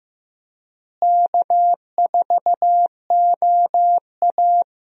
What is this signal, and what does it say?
Morse code, heard as a keyed tone, about 15 wpm, 700 Hz. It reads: K4OA
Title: USA Call Signs - 4 Characters - Repeated 2x - 15wpm